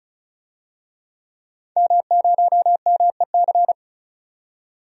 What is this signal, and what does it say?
Morse code, heard as a keyed tone, about 36 wpm, 700 Hz. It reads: M0MEC
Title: DXCC-100 Call Signs - 5 Characters - 35wpm